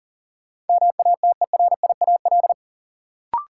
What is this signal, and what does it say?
Morse code, heard as a keyed tone, about 40 wpm, 700 Hz. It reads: MATERIAL
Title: Sentences from Top 1000 Words - Review Part 5 - New Word Drill - 40wpm